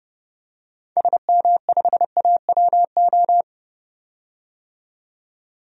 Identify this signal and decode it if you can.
Morse code, heard as a keyed tone, about 30 wpm, 700 Hz. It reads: SM5AWO
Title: DXCC-100 Call Signs - 6 Characters - 30wpm